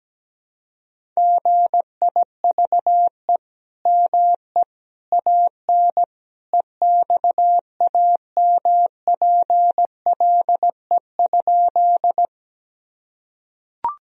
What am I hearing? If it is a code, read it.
Morse code, heard as a keyed tone, about 17 wpm, 700 Hz. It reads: GIVE ME AN EXAMPLE?
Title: Sentences from Top 1000 Words - Review Part 1 - 17wpm